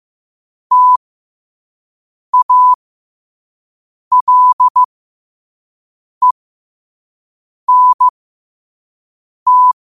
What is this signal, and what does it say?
Morse code, heard as a keyed tone, about 15 wpm, spaced as for 5 wpm, 1000 Hz. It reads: TALENT
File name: qotdfiles / 5wpm